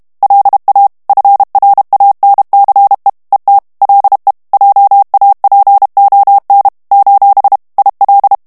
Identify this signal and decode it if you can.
Morse code, heard as a keyed tone, about 32 wpm, 800 Hz. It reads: LA FRANCE ET LE JAPON : IL
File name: morse mp3 / FRANCAIS TEXTE CLAIR 1 VITESSE 30 MOTS MINUTES